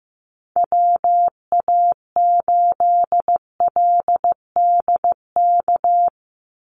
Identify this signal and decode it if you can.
Morse code, heard as a keyed tone, about 15 wpm, 700 Hz. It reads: WA8LDK
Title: USA Call Signs 15wpm